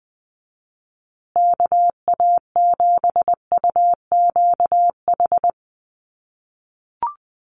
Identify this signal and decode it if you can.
Morse code, heard as a keyed tone, about 20 wpm, 700 Hz. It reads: KA7UQH
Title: USA Call Signs Repeated 2x 20wpm